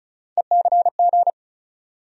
Morse code, heard as a keyed tone, about 36 wpm, 700 Hz. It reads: ECG
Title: USA Call Sign Postfixes - 3 Letters 35wpm